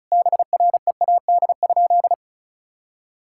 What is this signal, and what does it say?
Morse code, heard as a keyed tone, about 36 wpm, 700 Hz. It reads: BREAD?
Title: Sentences from Top 1600 Words 35wpm